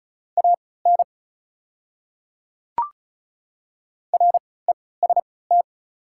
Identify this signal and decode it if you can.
Morse code, heard as a keyed tone, about 34 wpm, spaced as for 20 wpm, 690 Hz. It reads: AN REST
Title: 3x Character Spacing - Top 500 Words - 35wpm